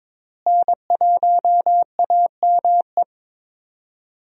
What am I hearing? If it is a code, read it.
Morse code, heard as a keyed tone, about 22 wpm, 700 Hz. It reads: N1AME
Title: USA Call Signs - 5 Characters - Repeated 2x - 22wpm